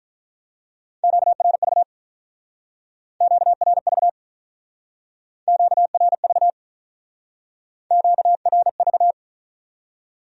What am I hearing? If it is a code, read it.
Morse code, heard as a keyed tone, about 43 wpm, 700 Hz. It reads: QRV QRV QRV QRV
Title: Speed-Racing Q-Codes - 35wpm